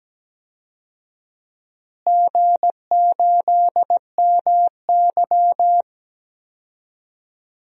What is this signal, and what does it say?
Morse code, heard as a keyed tone, about 17 wpm, 700 Hz. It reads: G8MY
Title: DXCC-100 Call Signs - 4 Characters - Repeated 3x - 17wpm